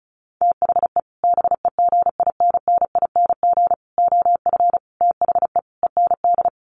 Morse code, heard as a keyed tone, about 34 wpm, 700 Hz. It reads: THE BEGINNING OF THE END
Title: Sets of 5 Words Encore2 35wpm